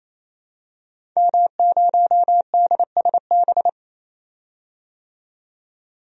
Morse code, heard as a keyed tone, about 27 wpm, 700 Hz. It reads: M0DSB
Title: DXCC-100 Call Signs - 5 Characters - 28wpm